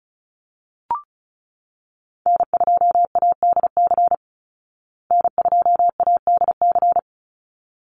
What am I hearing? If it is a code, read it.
Morse code, heard as a keyed tone, about 34 wpm, 700 Hz. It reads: N2ADC N2ADC
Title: Call Signs Repeated 3x - 35wpm